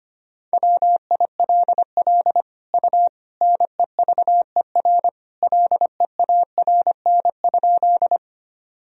Morse code, heard as a keyed tone, about 25 wpm, 700 Hz. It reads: WILL U NEVER LEARN?
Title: Sentences from Top 300 Words and Abbreviations 25wpm